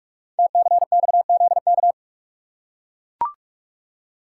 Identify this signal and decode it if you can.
Morse code, heard as a keyed tone, about 44 wpm, 700 Hz. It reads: TCXZK
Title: Code Groups - 5 Letters - 45wpm